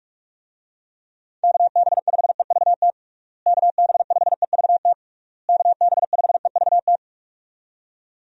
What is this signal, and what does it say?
Morse code, heard as a keyed tone, about 46 wpm, 700 Hz. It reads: KB5EVT KB5EVT KB5EVT
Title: USA Call Signs - 6 Characters - Repeated 3x - 45wpm